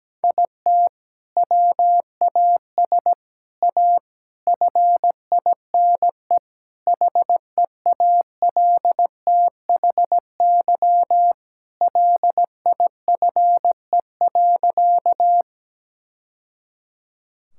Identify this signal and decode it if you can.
Morse code, heard as a keyed tone, about 17 wpm, 700 Hz. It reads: IT WAS A FINE HEALTHY LIFE.
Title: Sentences from Top 1000 Words - Review Part 5 - 17wpm